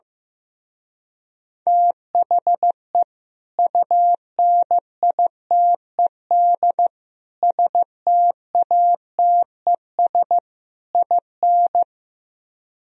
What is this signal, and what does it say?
Morse code, heard as a keyed tone, about 15 wpm, 700 Hz. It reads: THE UNITED STATES IN
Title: Speed-Racing - Sets of 4 Words 15wpm